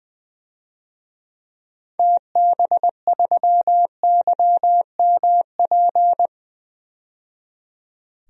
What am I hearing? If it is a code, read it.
Morse code, heard as a keyed tone, about 20 wpm, 700 Hz. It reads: TB3YMP
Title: DXCC-100 Call Signs - 6 Characters - 20wpm